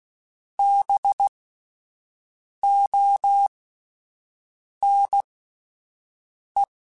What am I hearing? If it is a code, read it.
Morse code, heard as a keyed tone, about 16 wpm, spaced as for 5 wpm, 800 Hz. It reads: BONE